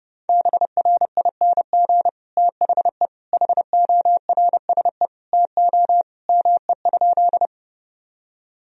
Morse code, heard as a keyed tone, about 30 wpm, 700 Hz. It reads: BRING THE HORSE TO ME?
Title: Sentences from Top 1000 Words - Review Part 3 - 30wpm